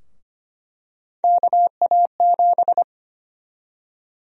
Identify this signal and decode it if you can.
Morse code, heard as a keyed tone, about 25 wpm, 700 Hz. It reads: KA7
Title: USA Call Sign Prefixes - Repeated 2x - 25wpm